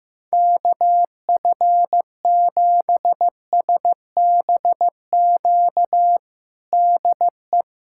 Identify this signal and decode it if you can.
Morse code, heard as a keyed tone, about 15 wpm, 700 Hz. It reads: KF7SBQ DE
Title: USA Callsign DE Callsign 15wpm